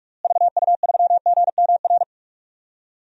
Morse code, heard as a keyed tone, about 45 wpm, 700 Hz. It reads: VU3GMR
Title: DXCC-100 Call Signs - 6 Characters - Repeated 2x - 45wpm